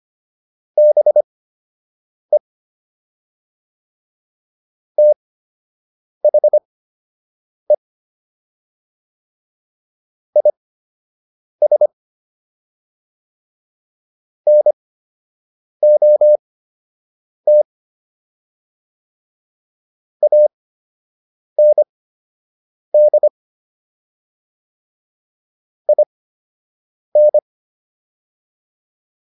Morse code, heard as a keyed tone, about 25 wpm, spaced as for 7 wpm, 600 Hz. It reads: BE THE IS NOT AND IN